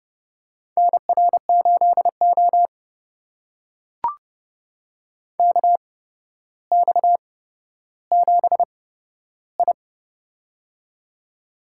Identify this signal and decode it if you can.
Morse code, heard as a keyed tone, about 30 wpm, spaced as for 13 wpm, 700 Hz. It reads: NR8O KX7I
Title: USA Call Signs - 4 Characters - 8x Character Spacing - 30wpm